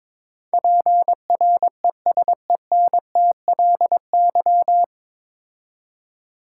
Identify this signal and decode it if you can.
Morse code, heard as a keyed tone, about 22 wpm, 700 Hz. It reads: PRESENTLY
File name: 9 Letter Words - First 3 Letters Given - 22wpm